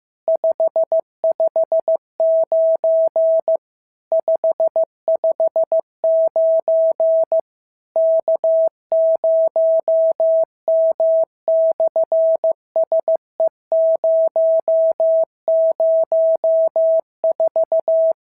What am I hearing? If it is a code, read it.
Morse code, heard as a keyed tone, about 15 wpm, 650 Hz. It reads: 559 559 K0M/SE004
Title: SOTA QSO Practice 15wpm